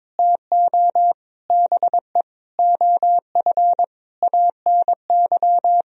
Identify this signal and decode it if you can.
Morse code, heard as a keyed tone, about 22 wpm, 700 Hz. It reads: TO BE OF ANY
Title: Sentences from Top 1000 Words - Review Part 4 - 22wpm